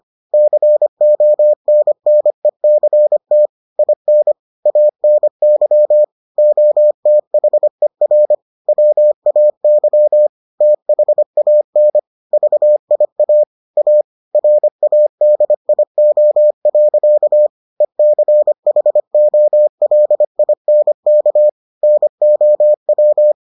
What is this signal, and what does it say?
Morse code, heard as a keyed tone, about 25 wpm, 600 Hz. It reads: CONNECT IN ANY OTHER WAY THAN VIA A RADIO. ECHOLINK NOW